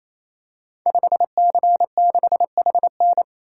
Morse code, heard as a keyed tone, about 27 wpm, 700 Hz. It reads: 5C6HN
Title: Code Groups - 5 Letter:Numbers - 28wpm